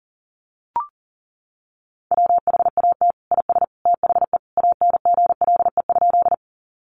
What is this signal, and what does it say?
Morse code, heard as a keyed tone, about 40 wpm, 700 Hz. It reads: WHAT IS THE ANGLE?
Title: Sentences from Top 1500 Words 40wpm